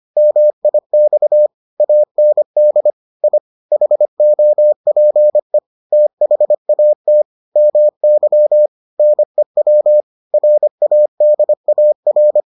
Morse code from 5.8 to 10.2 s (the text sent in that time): THAT MY NEW